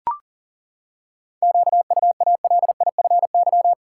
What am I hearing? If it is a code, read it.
Morse code, heard as a keyed tone, about 40 wpm, 700 Hz. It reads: QUALIFY
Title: Seven Letter Words - 40wpm